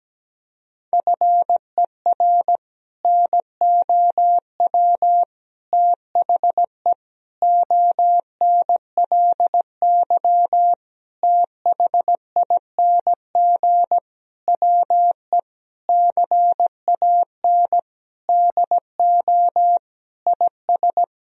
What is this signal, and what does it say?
Morse code, heard as a keyed tone, about 17 wpm, 700 Hz. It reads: FER NOW THE ONLY THING WE CAN DO IS
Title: Sentences from Top 200 Words and Abbreviations 17wpm